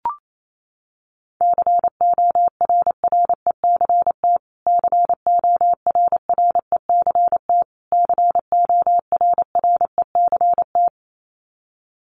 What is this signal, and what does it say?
Morse code, heard as a keyed tone, about 28 wpm, 700 Hz. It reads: CORRECT CORRECT CORRECT
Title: Top 500 Words Repeated 28wpm